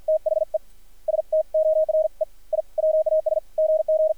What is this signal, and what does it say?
Morse code, heard as a keyed tone, about 53 wpm, 590 Hz. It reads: THE STORE I WAS GG
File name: archived / CRSnet